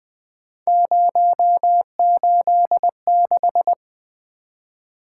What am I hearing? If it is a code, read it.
Morse code, heard as a keyed tone, about 20 wpm, 700 Hz. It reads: 086
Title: Three Letter-Number Combinations - 20wpm